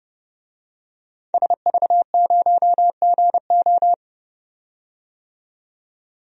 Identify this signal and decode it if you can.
Morse code, heard as a keyed tone, about 32 wpm, 700 Hz. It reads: SV0GO
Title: DXCC-100 Call Signs - 5 Characters - 30wpm